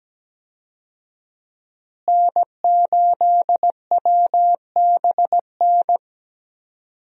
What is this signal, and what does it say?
Morse code, heard as a keyed tone, about 17 wpm, 700 Hz. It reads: N8WBN
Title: USA Call Signs - 5 Characters - Repeated 2x - 17wpm